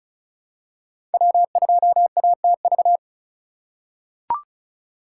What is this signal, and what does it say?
Morse code, heard as a keyed tone, about 34 wpm, 700 Hz. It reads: W2ATV
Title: USA Call Signs - 5 Characters - 4x Character Spacing - 35wpm